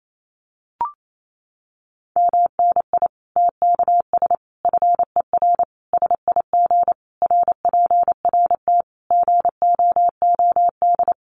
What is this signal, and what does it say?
Morse code, heard as a keyed tone, about 28 wpm, 700 Hz. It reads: MNI TKS FER SIG RPRT GOOD
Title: Common QSO Phrases 28wpm